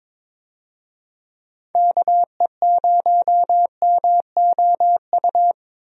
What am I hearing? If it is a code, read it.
Morse code, heard as a keyed tone, about 22 wpm, 700 Hz. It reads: KE0MOU
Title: USA Call Signs - 6 Characters - 22wpm